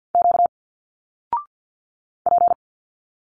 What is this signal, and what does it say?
Morse code, heard as a keyed tone, about 50 wpm, 700 Hz. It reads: Q P
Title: Single Letters 50wpm